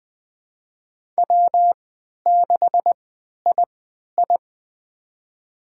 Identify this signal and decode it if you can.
Morse code, heard as a keyed tone, about 21 wpm, spaced as for 12 wpm, 700 Hz. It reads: W6II
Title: USA Call Signs - 4 Characters - 3x Character Spacing - 20wpm